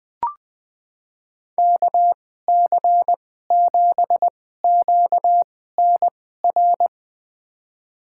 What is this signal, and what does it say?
Morse code, heard as a keyed tone, about 20 wpm, spaced as for 15 wpm, 700 Hz. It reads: KC7QNR
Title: USA Call Signs - 2x Character Spacing - 20wpm